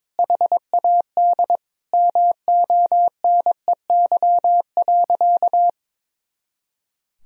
Morse code, heard as a keyed tone, about 22 wpm, 700 Hz. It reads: HAD MONEY.
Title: Sentences from Top 500 Words and Abbreviations 22wpm